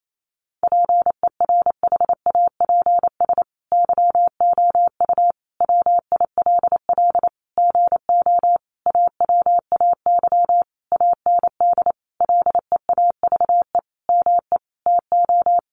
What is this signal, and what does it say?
Morse code, heard as a keyed tone, about 28 wpm, 700 Hz. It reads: PERHAPS YOU WILL GO AWAY AND LEAVE ME TO
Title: Sentences from Top 1300 Words 28wpm